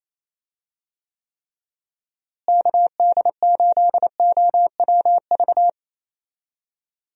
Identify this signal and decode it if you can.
Morse code, heard as a keyed tone, about 28 wpm, 700 Hz. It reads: KD8OWV
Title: USA Call Signs - 6 Characters - Repeated 2x - 28wpm